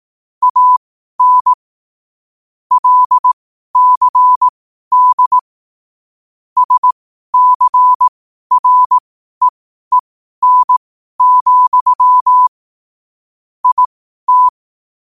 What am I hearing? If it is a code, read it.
Morse code, heard as a keyed tone, about 18 wpm, spaced as for 12 wpm, 1000 Hz. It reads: AN LCD SCREEN, IT
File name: qotdfiles / 12wpm